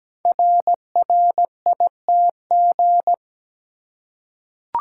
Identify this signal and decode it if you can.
Morse code, heard as a keyed tone, about 17 wpm, 700 Hz. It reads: RRITG
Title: Code Groups - 5 Letter:Numbers - 17wpm